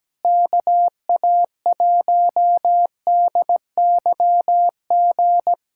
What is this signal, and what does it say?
Morse code, heard as a keyed tone, about 17 wpm, 700 Hz. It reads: KA1DYG
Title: USA Callsign DE Callsign - 6 Characters - 17wpm